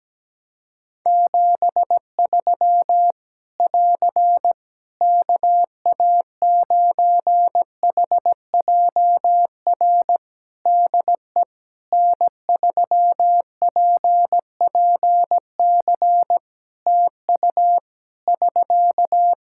Morse code, heard as a keyed tone, about 17 wpm, 700 Hz. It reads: 73 <AR> KA9HJR DE N3PPC TU <SK>
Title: Practice QSO - Exchange 4 - Ending - 17wpm